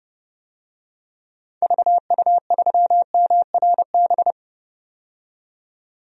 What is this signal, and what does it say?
Morse code, heard as a keyed tone, about 30 wpm, 700 Hz. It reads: VU3MRB
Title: DXCC-100 Call Signs - 6 Characters - 30wpm